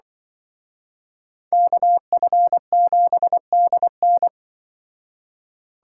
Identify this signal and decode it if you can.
Morse code, heard as a keyed tone, about 25 wpm, 700 Hz. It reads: KF7DN
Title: Speed Racing - USA 2x2 Call Signs 20wpm